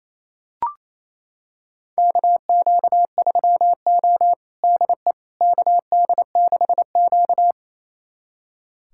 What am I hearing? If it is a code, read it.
Morse code, heard as a keyed tone, about 28 wpm, 700 Hz. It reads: KQ3O DE KD6Q
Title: USA Callsign DE Callsign - 4 Characters - 28wpm